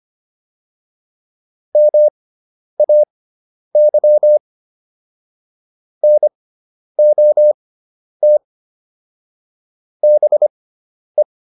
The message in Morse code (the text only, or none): MAY NOT BE